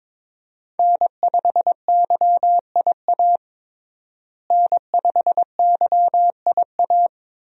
Morse code, heard as a keyed tone, about 21 wpm, 700 Hz. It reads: N5YIA N5YIA
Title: USA Call Signs - 5 Characters - Repeated 3x - 22wpm